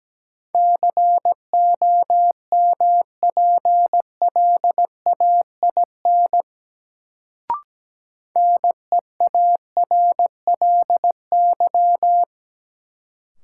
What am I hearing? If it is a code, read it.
Morse code, heard as a keyed tone, about 17 wpm, 700 Hz. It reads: COMPLAIN NEARLY
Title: Sentences from Top 1300 Words - New Word Drill Alt - 17wpm